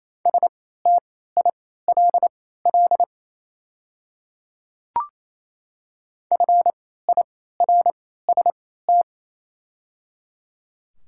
Morse code, heard as a keyed tone, about 28 wpm, spaced as for 16 wpm, 700 Hz. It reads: STILL FIRST
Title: 3x Character Spacing - Top 300 Words - 28wpm